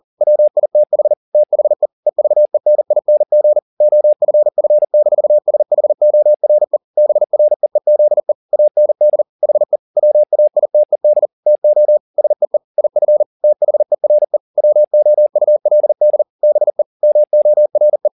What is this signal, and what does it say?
Morse code, heard as a keyed tone, about 40 wpm, 600 Hz. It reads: WITH THE EVENING OFF-SHORE BREEZE AND HE WAITED TO SEE IF THERE WOULD BE MORE
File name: the old man and the sea 2 40wpm